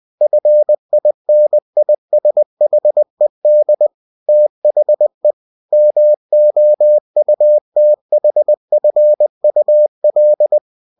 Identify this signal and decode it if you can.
Morse code, heard as a keyed tone, about 20 wpm, 600 Hz. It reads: FINISHED THE MOUTHFUL
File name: cw-books / winnie-the-pooh / winnie-20wpm-page-0033